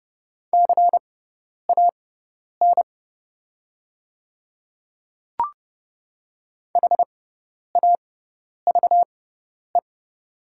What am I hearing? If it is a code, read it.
Morse code, heard as a keyed tone, about 29 wpm, spaced as for 10 wpm, 700 Hz. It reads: CAN HAVE
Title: 6x Character Spacing - Top 100 Words - 30wpm